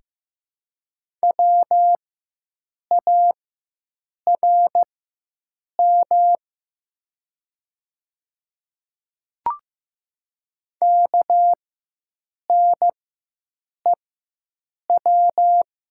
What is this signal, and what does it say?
Morse code, heard as a keyed tone, about 15 wpm, spaced as for 7 wpm, 700 Hz. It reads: WARM KNEW
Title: 4x Character Spacing - Top 500 Words - 15wpm